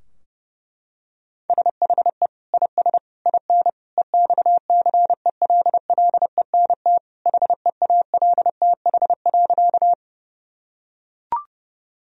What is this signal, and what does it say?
Morse code, heard as a keyed tone, about 30 wpm, 700 Hz. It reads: SHE IS IN EXCELLENT HEALTH.
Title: Sentences from Top 1000 Words - Review Part 3 - 30wpm